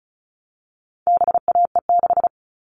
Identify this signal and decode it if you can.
Morse code, heard as a keyed tone, about 34 wpm, 700 Hz. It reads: BAE6
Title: All Cut Numbers - 4-Digit 35wpm